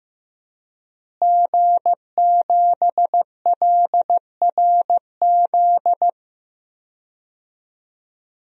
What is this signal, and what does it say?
Morse code, heard as a keyed tone, about 15 wpm, 700 Hz. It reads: G7LRZ
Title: DXCC-100 Call Signs - 5 Characters - Repeated 2x - 15wpm